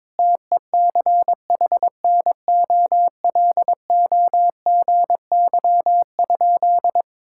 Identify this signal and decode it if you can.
Morse code, heard as a keyed tone, about 22 wpm, 700 Hz. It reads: TECHNOLOGY?
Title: Sentences from Top 1000 Words - Review Part 5 - 22wpm